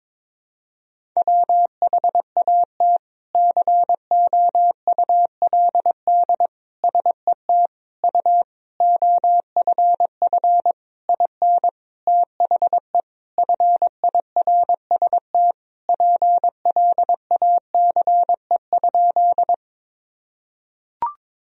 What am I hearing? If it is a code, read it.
Morse code, heard as a keyed tone, about 22 wpm, 700 Hz. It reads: WHAT COULD SET U OFF IN THE FIRST PLACE?
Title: Sentences from Top 200 Words and Abbreviations 22wpm